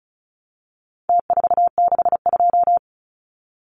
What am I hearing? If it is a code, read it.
Morse code, heard as a keyed tone, about 34 wpm, 700 Hz. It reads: T462
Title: Numbers - Contest Style - 4-Digit 35wpm